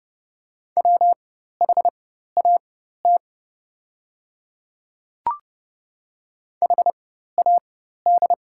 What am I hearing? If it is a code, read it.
Morse code, heard as a keyed tone, about 30 wpm, spaced as for 14 wpm, 700 Hz. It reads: WHAT HAD
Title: 4x Character Spacing - Top 100 Words - 30wpm